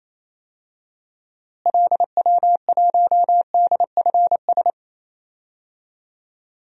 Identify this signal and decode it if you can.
Morse code, heard as a keyed tone, about 27 wpm, 700 Hz. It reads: LW1DFS
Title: DXCC-100 Call Signs - 6 Characters - Repeated 2x - 28wpm